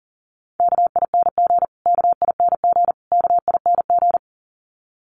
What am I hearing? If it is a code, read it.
Morse code, heard as a keyed tone, about 40 wpm, 700 Hz. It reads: KING KING KING
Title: Top 500 Words Repeated 40wpm